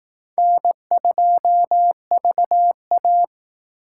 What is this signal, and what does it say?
Morse code, heard as a keyed tone, about 18 wpm, 700 Hz. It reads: N2VA
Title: Speed Racing - USA 1x2 Call Signs 15wpm